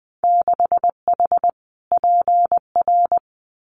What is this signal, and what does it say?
Morse code, heard as a keyed tone, about 20 wpm, 700 Hz. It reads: 6H PR
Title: Winter Field Day Exchanges Encore 20wpm